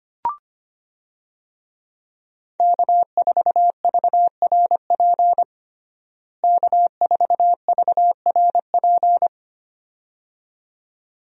Morse code, heard as a keyed tone, about 25 wpm, 700 Hz. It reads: K4VRP K4VRP
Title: USA Call Signs - 5 Characters - Repeated 2x - 25wpm